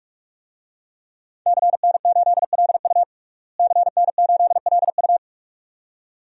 Morse code, heard as a keyed tone, about 46 wpm, 700 Hz. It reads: CN8LU CN8LU
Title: DXCC-100 Call Signs - 5 Characters - Repeated 2x - 45wpm